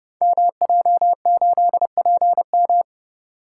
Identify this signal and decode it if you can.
Morse code, heard as a keyed tone, about 28 wpm, 700 Hz. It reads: MJ8PM
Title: Code Groups - 5 Letter:Numbers - 30wpm